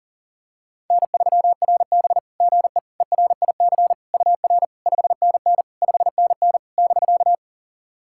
Morse code, heard as a keyed tone, about 40 wpm, 700 Hz. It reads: N3RB GE ERIC UR 5NN 5NN <BK>
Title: SOTA QSO Practice 40wpm